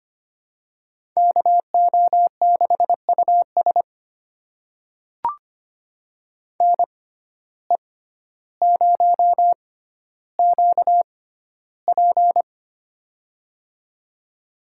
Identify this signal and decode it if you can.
Morse code, heard as a keyed tone, about 25 wpm, spaced as for 13 wpm, 700 Hz. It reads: KO6US NE0QP
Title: USA Call Signs - 6x Character Spacing - 25wpm